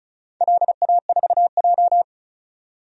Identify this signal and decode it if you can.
Morse code, heard as a keyed tone, about 34 wpm, 700 Hz. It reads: LA4J
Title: DXCC-100 Call Signs - 4 Characters - 35wpm